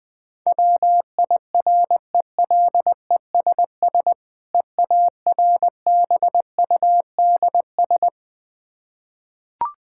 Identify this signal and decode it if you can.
Morse code, heard as a keyed tone, about 20 wpm, 700 Hz. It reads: WIRELESS EARBUDS
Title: Difficult Words with Hints - 20wpm